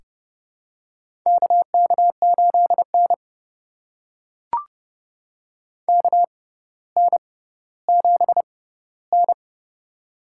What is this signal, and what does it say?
Morse code, heard as a keyed tone, about 32 wpm, spaced as for 16 wpm, 700 Hz. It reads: KK8N KN7N
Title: USA Call Signs - 4 Characters - 6x Character Spacing - 30wpm